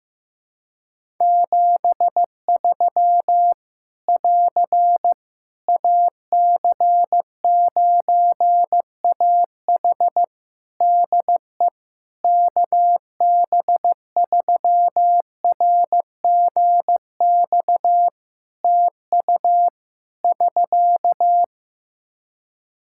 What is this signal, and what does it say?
Morse code, heard as a keyed tone, about 15 wpm, 700 Hz. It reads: 73 <AR> AC9AH DE KB3RGX TU <SK>
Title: Practice QSO - Exchange 4 - Ending - 15wpm